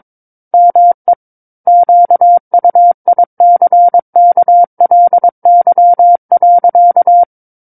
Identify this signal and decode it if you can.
Morse code, heard as a keyed tone, about 22 wpm, 700 Hz. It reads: ME QUICKLY.
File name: Book One / 13-Chapter 13 - 22 WPM